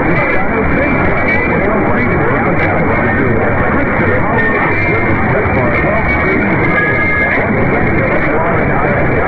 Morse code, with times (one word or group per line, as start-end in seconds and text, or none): none